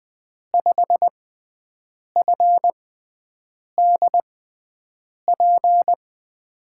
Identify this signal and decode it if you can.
Morse code, heard as a keyed tone, about 20 wpm, spaced as for 7 wpm, 700 Hz. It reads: 5FDP
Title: USA Call Signs - 6x Character Spacing - 20wpm